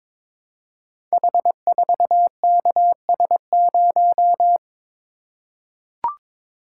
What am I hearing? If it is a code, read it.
Morse code, heard as a keyed tone, about 23 wpm, 700 Hz. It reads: H4KS0
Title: Code Groups - 5 Letter:Numbers - 22wpm